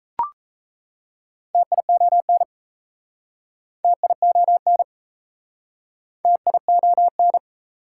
Morse code, heard as a keyed tone, about 38 wpm, 700 Hz. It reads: TION TION TION